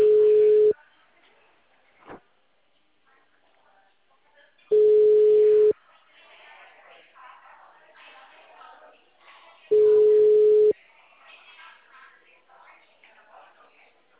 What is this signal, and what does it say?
Ringback tone (Europe, 425 Hz), with speech.